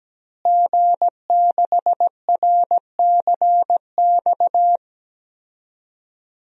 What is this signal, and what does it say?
Morse code, heard as a keyed tone, about 17 wpm, 700 Hz. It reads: G6RCX
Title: DXCC-100 Call Signs - 5 Characters - Repeated 3x - 17wpm